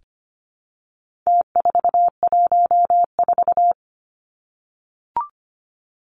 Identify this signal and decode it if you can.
Morse code, heard as a keyed tone, about 25 wpm, 700 Hz. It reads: T414
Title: Numbers - Contest Style - 4-Digit 25wpm